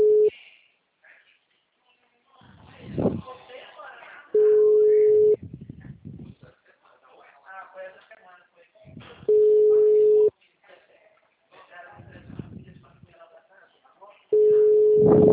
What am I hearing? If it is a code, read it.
Ringback tone (Europe, 425 Hz), with speech.
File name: out- -2022- - - .90463